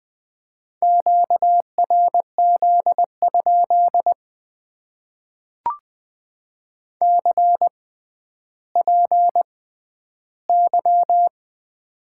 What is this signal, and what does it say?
Morse code, heard as a keyed tone, about 20 wpm, spaced as for 11 wpm, 700 Hz. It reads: QRZ? CPY
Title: Top 90 QSO Elements - 6x Character Spacing - 20wpm